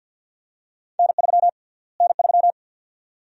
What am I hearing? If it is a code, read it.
Morse code, heard as a keyed tone, about 48 wpm, 700 Hz. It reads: N3 N3
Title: USA Call Sign Prefixes - Repeated 2x - 50wpm